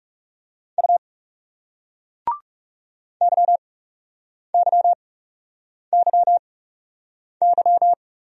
Morse code, heard as a keyed tone, about 40 wpm, spaced as for 8 wpm, 700 Hz. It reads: U YYYY